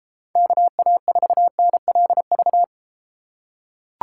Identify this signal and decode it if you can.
Morse code, heard as a keyed tone, about 33 wpm, 700 Hz. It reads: KA4NLV